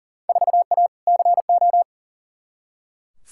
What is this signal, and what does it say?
Morse code, heard as a keyed tone, about 40 wpm, 700 Hz. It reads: 4A CO
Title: Field Day Exchanges - 40wpm